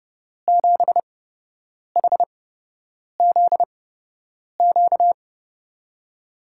Morse code, heard as a keyed tone, about 29 wpm, spaced as for 8 wpm, 700 Hz. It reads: 7HZQ
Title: USA Call Signs - 5 Characters - 8x Character Spacing - 30wpm